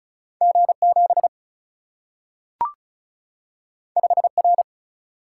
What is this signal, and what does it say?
Morse code, heard as a keyed tone, about 36 wpm, 700 Hz. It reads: G 7 5 R
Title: Single-Letter Single-Number 35wpm